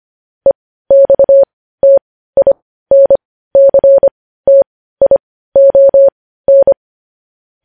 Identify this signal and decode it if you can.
Morse code, heard as a keyed tone, about 25 wpm, spaced as for 15 wpm, 550 Hz. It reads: EXTINCTION
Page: News headlines at 15